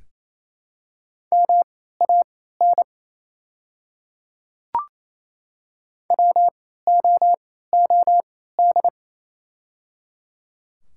Morse code, heard as a keyed tone, about 27 wpm, spaced as for 16 wpm, 700 Hz. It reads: MAN WOOD
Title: 3x Character Spacing - Top 300 Words - 28wpm